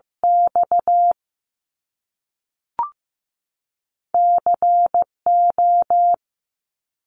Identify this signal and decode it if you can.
Morse code, heard as a keyed tone, about 15 wpm, 700 Hz. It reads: X C O
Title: Top Two Letter Combinations in Words 15wpm